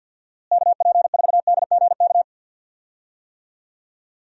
Morse code, heard as a keyed tone, about 49 wpm, 700 Hz. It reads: KP4DGK
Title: USA Call Signs - 6 Characters - Repeated 2x - 50wpm